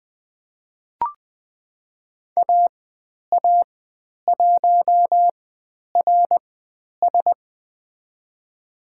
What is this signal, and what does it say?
Morse code, heard as a keyed tone, about 20 wpm, spaced as for 10 wpm, 700 Hz. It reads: AA1RS
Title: 109 Lesson 18g - Letter m - Call Signs 20wpm